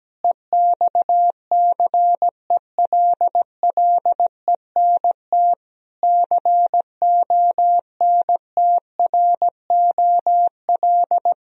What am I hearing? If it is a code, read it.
Morse code, heard as a keyed tone, about 17 wpm, 700 Hz. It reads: EXCELLENT CONTROL
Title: Sentences from Top 1000 Words - Review Part 3 - 17wpm